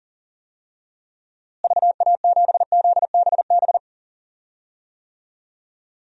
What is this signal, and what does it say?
Morse code, heard as a keyed tone, about 40 wpm, 700 Hz. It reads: VA7ZBB
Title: DXCC-100 Call Signs - 6 Characters - Repeated 2x - 40wpm